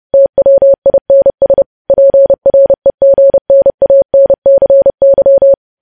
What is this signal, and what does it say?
Morse code, heard as a keyed tone, about 30 wpm, 550 Hz. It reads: TWINS PREGNANCY
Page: news headlines in morse code at 30 wpm